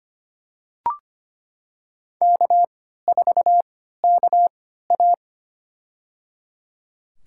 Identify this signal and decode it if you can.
Morse code, heard as a keyed tone, about 25 wpm, spaced as for 14 wpm, 700 Hz. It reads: K4KA
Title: USA Call Signs - 4 Characters - 3x Character Spacing - 25wpm